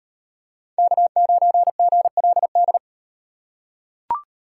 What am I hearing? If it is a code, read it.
Morse code, heard as a keyed tone, about 37 wpm, 700 Hz. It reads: K9GLD